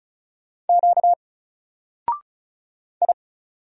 Morse code, heard as a keyed tone, about 35 wpm, 700 Hz. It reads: Q I
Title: Single Letter-Number Alt 35wpm